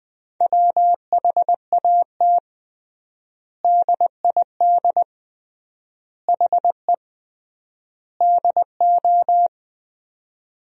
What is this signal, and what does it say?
Morse code, heard as a keyed tone, about 21 wpm, 700 Hz. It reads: WHAT DID HE DO
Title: 3x Word-Spacing - Sentences from Top 100 Words - 20wpm